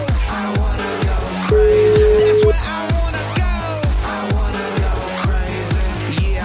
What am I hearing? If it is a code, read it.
Ringback tone (Europe, 425 Hz), with noise.